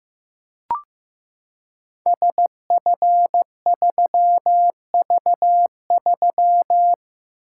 Morse code, heard as a keyed tone, about 15 wpm, 700 Hz. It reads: SF3V3
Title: Code Groups - 5 Letter:Numbers - 15wpm